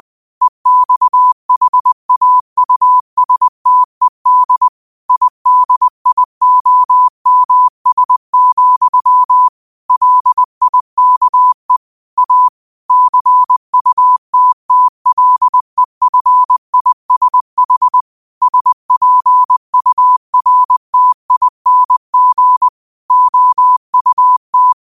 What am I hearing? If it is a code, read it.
Morse code, heard as a keyed tone, about 20 wpm, 1000 Hz. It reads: EXHAUSTED IDIOMS, LIKE A CUTTLEFISH SPURTING OUT
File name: qotdfiles / 20wpm